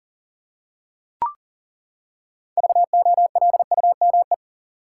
Morse code, heard as a keyed tone, about 40 wpm, 700 Hz. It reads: VOLUME
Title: Six Letter Words - 40wpm